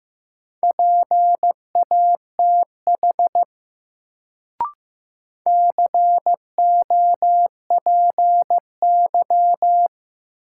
Morse code, heard as a keyed tone, about 15 wpm, 700 Hz. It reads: PATH COPY
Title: Sentences from Top 1400 Words - New Word Drill - 15wpm